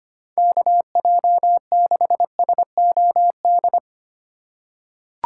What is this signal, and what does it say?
Morse code, heard as a keyed tone, about 25 wpm, 700 Hz. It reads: KJ6SOD
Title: USA Call Signs - 6 Characters - 3x Character Spacing - 25wpm